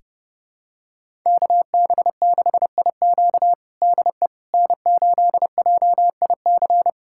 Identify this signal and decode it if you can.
Morse code, heard as a keyed tone, about 30 wpm, 700 Hz. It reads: KB6IQ DE N8JIC
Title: USA Callsign DE Callsign - 5 Characters - 30wpm